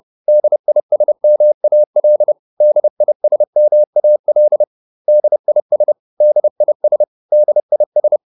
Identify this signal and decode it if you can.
Morse code, heard as a keyed tone, about 30 wpm, 600 Hz. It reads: DISMAL DISMAL DIS DIS DIS